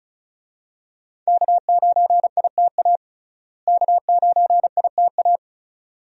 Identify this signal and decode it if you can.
Morse code, heard as a keyed tone, about 36 wpm, 700 Hz. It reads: K9ITA K9ITA
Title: USA Call Signs - 5 Characters - Repeated 2x - 35wpm